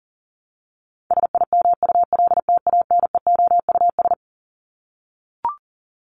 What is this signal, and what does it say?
Morse code, heard as a keyed tone, about 40 wpm, 700 Hz. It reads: SIMULTANEOUS
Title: LY Suffix - Top 100 Words with Base Words 40wpm